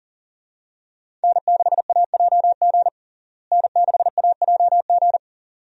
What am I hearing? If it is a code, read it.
Morse code, heard as a keyed tone, about 40 wpm, 700 Hz. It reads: N6AJG N6AJG
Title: USA Call Signs - 5 Characters - Repeated 2x - 40wpm